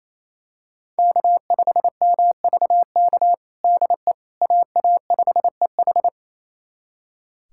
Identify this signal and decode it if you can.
Morse code, heard as a keyed tone, about 27 wpm, 700 Hz. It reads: K5MVK DE AA5EH
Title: USA Callsign DE Callsign - 5 Characters - 28wpm